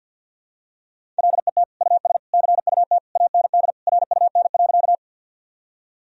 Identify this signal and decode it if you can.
Morse code, heard as a keyed tone, about 50 wpm, 700 Hz. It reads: LET US CUT AND RUN.
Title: Sentences from Top 1000 Words - Review Part 1 - 50wpm